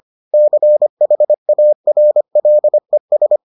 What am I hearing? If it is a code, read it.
Morse code, heard as a keyed tone, about 25 wpm, 600 Hz. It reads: CHARLES